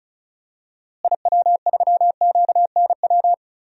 Morse code, heard as a keyed tone, about 34 wpm, 700 Hz. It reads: IW3QNW
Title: DXCC-100 Call Signs - 6 Characters - Repeated 3x - 35wpm